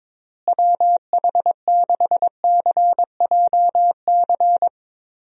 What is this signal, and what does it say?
Morse code, heard as a keyed tone, about 22 wpm, 700 Hz. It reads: WH6CJC
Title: USA Call Signs - 6 Characters - Repeated 3x - 22wpm